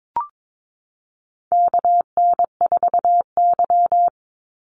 Morse code, heard as a keyed tone, about 22 wpm, 700 Hz. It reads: KN4Y
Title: Call Signs Repeated 3x - 22wpm